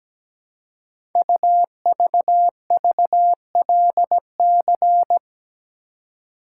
Morse code, heard as a keyed tone, about 17 wpm, 700 Hz. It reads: UVVLC
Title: Code Groups - 5 Letters - 17wpm